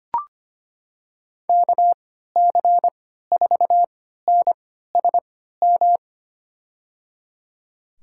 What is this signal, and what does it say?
Morse code, heard as a keyed tone, about 25 wpm, spaced as for 14 wpm, 700 Hz. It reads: KC4NSM
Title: USA Call Signs - 6 Characters - 3x Character Spacing - 25wpm